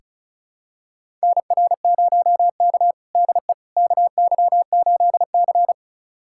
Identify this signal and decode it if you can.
Morse code, heard as a keyed tone, about 36 wpm, 700 Hz. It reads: NR0K DE KY8C
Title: USA Callsign DE Callsign - 4 Characters - 35wpm